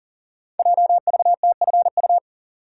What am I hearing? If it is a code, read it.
Morse code, heard as a keyed tone, about 41 wpm, 700 Hz. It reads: JVTFU
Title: Code Groups - 5 Letter:Numbers - 40wpm